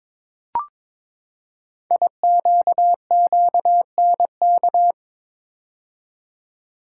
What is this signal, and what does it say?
Morse code, heard as a keyed tone, about 22 wpm, 700 Hz. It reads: IQQNK
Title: Code Groups - 5 Letters - 22wpm